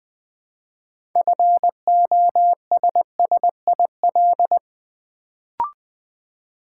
Morse code, heard as a keyed tone, about 20 wpm, 700 Hz. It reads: FOSSIL
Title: Build A Word - 2 Letters At A Time - 6-Letter Words - 20wpm